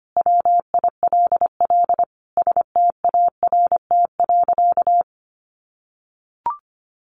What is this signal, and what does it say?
Morse code, heard as a keyed tone, about 25 wpm, 700 Hz. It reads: WILL START.
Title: Sentences from Top 300 Words 25wpm